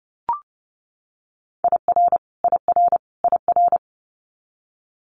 Morse code, heard as a keyed tone, about 30 wpm, 700 Hz. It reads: IR IR IR
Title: Multi-letter Phonemes Repeated 3x 30wpm